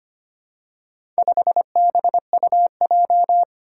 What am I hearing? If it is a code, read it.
Morse code, heard as a keyed tone, about 25 wpm, 700 Hz. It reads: 5BUJ
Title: Four Challenging Letter-Number Combinations 25wpm